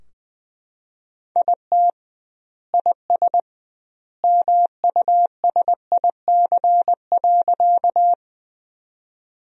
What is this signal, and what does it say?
Morse code, heard as a keyed tone, about 20 wpm, 700 Hz. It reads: IT IS MUSIC.
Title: 2x Word-Spacing - Sentences from Top 300 Words - 20wpm